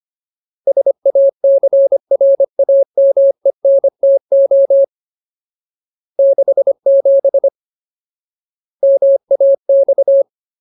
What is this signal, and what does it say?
Morse code, heard as a keyed tone, about 25 wpm, 550 Hz. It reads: SACRAMENTO 67 MAX
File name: Ultimate Word List / 25WPM Ultimate